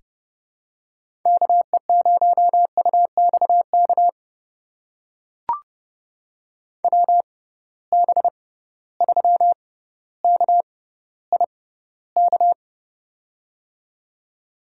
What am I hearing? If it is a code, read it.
Morse code, heard as a keyed tone, about 30 wpm, spaced as for 15 wpm, 700 Hz. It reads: KE0UXK WB3KIK
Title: USA Call Signs - 6 Characters - 6x Character Spacing - 30wpm